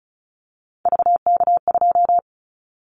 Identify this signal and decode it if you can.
Morse code, heard as a keyed tone, about 36 wpm, 700 Hz. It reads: VK2
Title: Callsign Prefixes for Top 100 DXCC - 35wpm